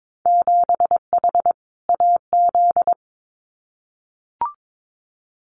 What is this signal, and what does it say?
Morse code, heard as a keyed tone, about 22 wpm, 700 Hz. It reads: 7H AZ
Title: Winter Field Day Exchanges Encore 22wpm